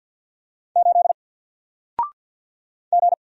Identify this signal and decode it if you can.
Morse code, heard as a keyed tone, about 48 wpm, 700 Hz. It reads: 8 G
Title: Single Letter-Number 50wpm